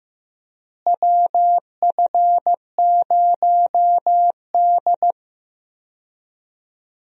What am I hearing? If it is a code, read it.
Morse code, heard as a keyed tone, about 15 wpm, 700 Hz. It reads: WF0D
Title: USA Call Signs - 4 Characters - Repeated 3x - 15wpm